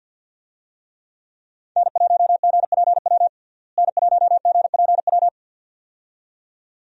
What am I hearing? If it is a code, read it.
Morse code, heard as a keyed tone, about 51 wpm, 700 Hz. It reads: N1GPW N1GPW
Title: USA Call Signs - 5 Characters - Repeated 2x - 50wpm